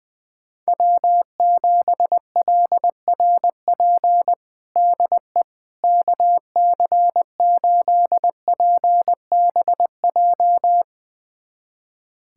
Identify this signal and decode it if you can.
Morse code, heard as a keyed tone, about 21 wpm, 700 Hz. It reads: W7LRP DE KC8PBJ
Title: USA Callsign DE Callsign 20wpm